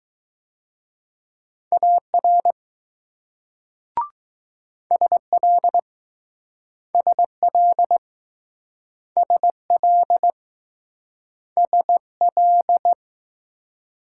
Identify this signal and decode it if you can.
Morse code, heard as a keyed tone, about 20 wpm, 700 Hz. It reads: AR SL SL SL SL